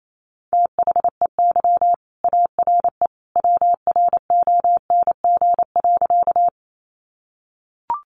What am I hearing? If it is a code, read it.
Morse code, heard as a keyed tone, about 28 wpm, 700 Hz. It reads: THEY ARE WRONG.
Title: Sentences from Top 900 Words 28wpm